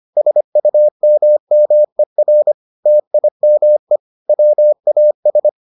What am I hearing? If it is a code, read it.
Morse code, heard as a keyed tone, about 25 wpm, 600 Hz. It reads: SUMMER TIME WAS